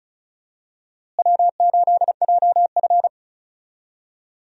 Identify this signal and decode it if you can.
Morse code, heard as a keyed tone, about 34 wpm, 700 Hz. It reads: W8JF
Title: USA Call Signs - 4 Characters - Repeated 2x - 35wpm